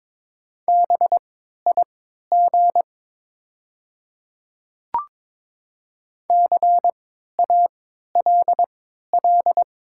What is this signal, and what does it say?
Morse code, heard as a keyed tone, about 22 wpm, spaced as for 12 wpm, 700 Hz. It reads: BIG CALL
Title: 3x Character Spacing - Top 300 Words - 22wpm